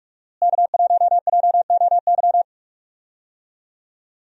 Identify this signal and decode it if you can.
Morse code, heard as a keyed tone, about 47 wpm, 700 Hz. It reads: K1JOY
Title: USA Call Signs - 5 Characters - Repeated 2x - 45wpm